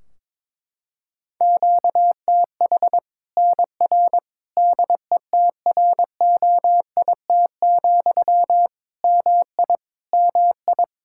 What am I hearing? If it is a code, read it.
Morse code, heard as a keyed tone, about 22 wpm, 700 Hz. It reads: QTH NR DETROIT, MI MI
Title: Practice QSO - Exchange 1 - Beginning - 22wpm